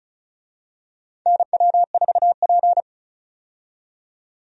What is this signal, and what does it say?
Morse code, heard as a keyed tone, about 34 wpm, 700 Hz. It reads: NW4P
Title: USA Call Signs - 4 Characters - 35wpm